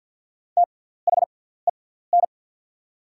Morse code, heard as a keyed tone, about 48 wpm, spaced as for 17 wpm, 700 Hz. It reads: THEN
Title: 6x Character Spacing - Top 100 Words - 50wpm